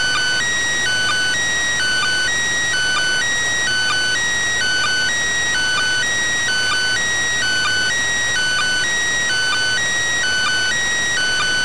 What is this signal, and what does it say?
There is no code.